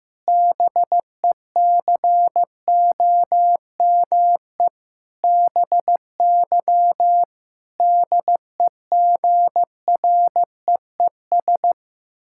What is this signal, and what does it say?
Morse code, heard as a keyed tone, about 15 wpm, 700 Hz. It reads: BECOME BY DEGREES
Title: Pangram Challenge v2 - 15wpm